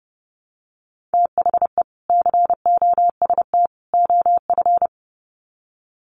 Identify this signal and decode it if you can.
Morse code, heard as a keyed tone, about 30 wpm, 700 Hz. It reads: THE COST OF
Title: Sets of 3 Words 30wpm